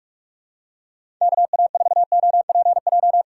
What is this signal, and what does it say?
Morse code, heard as a keyed tone, about 41 wpm, 700 Hz. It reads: KA4OPJ
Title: USA Call Signs - 6 Characters - 2x Character Spacing - 45wpm